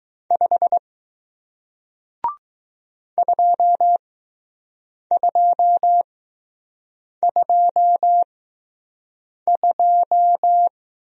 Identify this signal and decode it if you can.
Morse code, heard as a keyed tone, about 20 wpm, spaced as for 6 wpm, 700 Hz. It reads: 5 2222